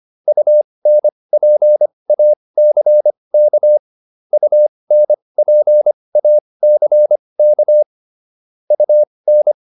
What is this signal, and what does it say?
Morse code, heard as a keyed tone, about 25 wpm, spaced as for 20 wpm, 600 Hz. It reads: UNPACK UNPACK UN
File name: UN6 20